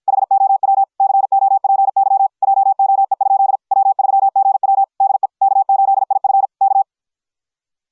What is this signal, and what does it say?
Morse code, heard as a keyed tone, about 53 wpm, 750 Hz. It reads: HOW COPY JOE? W3GW DE K8IF K